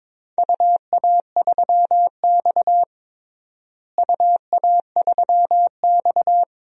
Morse code, heard as a keyed tone, about 21 wpm, 700 Hz. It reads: UA3X UA3X
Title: DXCC-100 Call Signs - 4 Characters - Repeated 2x - 22wpm